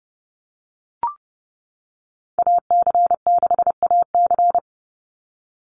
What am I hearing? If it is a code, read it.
Morse code, heard as a keyed tone, about 30 wpm, 700 Hz. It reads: AC6AC
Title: Call Signs - 30wpm